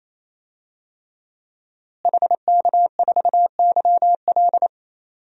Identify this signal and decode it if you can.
Morse code, heard as a keyed tone, about 28 wpm, 700 Hz. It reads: HK4YL
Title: DXCC-100 Call Signs - 5 Characters - Repeated 3x - 28wpm